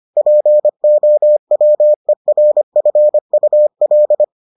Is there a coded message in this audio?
Yes: Morse code from 0.2 to 4.2 s, heard as a keyed tone, about 25 wpm, 600 Hz. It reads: POWERFUL